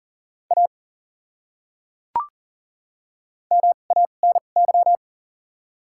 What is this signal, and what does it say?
Morse code, heard as a keyed tone, about 40 wpm, spaced as for 29 wpm, 700 Hz. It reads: A MANY
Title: 2x Character Spacing - Top 100 Words - 40wpm